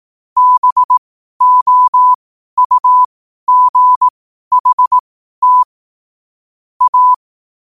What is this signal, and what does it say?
Morse code, heard as a keyed tone, about 18 wpm, spaced as for 12 wpm, 1000 Hz. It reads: BOUGHT A
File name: qotdfiles / 12wpm